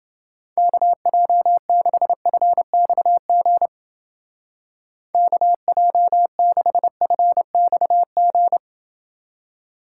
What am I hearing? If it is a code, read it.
Morse code, heard as a keyed tone, about 30 wpm, 700 Hz. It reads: KJ6FXG KJ6FXG